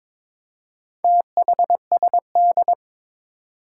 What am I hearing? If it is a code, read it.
Morse code, heard as a keyed tone, about 21 wpm, 700 Hz. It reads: THSD
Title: Four Challenging Letter-Number Combinations 22wpm